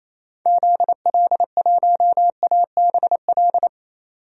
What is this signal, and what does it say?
Morse code, heard as a keyed tone, about 28 wpm, 700 Hz. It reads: ZL1ABL
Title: DXCC-100 Call Signs - 6 Characters - 28wpm